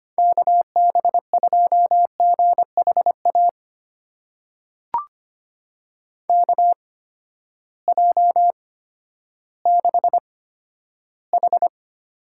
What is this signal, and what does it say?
Morse code, heard as a keyed tone, about 25 wpm, spaced as for 12 wpm, 700 Hz. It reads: KB2GHA KJ6H
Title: USA Call Signs - 6 Characters - 8x Character Spacing - 25wpm